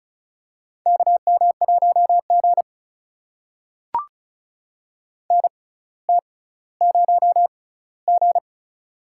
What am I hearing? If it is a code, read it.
Morse code, heard as a keyed tone, about 36 wpm, spaced as for 18 wpm, 700 Hz. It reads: KM1G NT0G
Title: USA Call Signs - 4 Characters - 6x Character Spacing - 35wpm